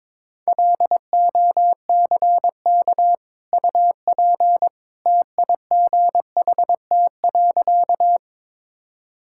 Morse code, heard as a keyed tone, about 22 wpm, 700 Hz. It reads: LOCK UP TIGHT.
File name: practice / 22wpm / Easy Sentences with Hints - 22wpm